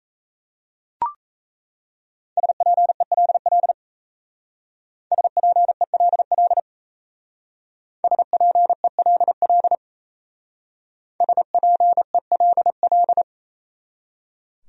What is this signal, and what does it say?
Morse code, heard as a keyed tone, about 32 wpm, 700 Hz. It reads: SPELL SPELL SPELL SPELL
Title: Speed-Racing Top 300 Words 28wpm